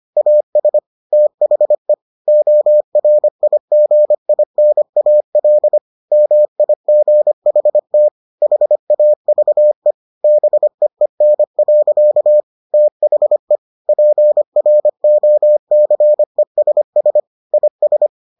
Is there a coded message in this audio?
Yes: Morse code from 0.2 to 18.1 s, heard as a keyed tone, about 25 wpm, 600 Hz. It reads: AS THE ORIGINAL MIGHT HAVE BEEN. THE PROCESS IS